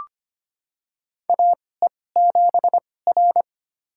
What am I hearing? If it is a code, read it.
Morse code, heard as a keyed tone, about 25 wpm, spaced as for 18 wpm, 700 Hz. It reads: AE7R
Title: USA Call Signs - 4 Characters - 2x Character Spacing - 25wpm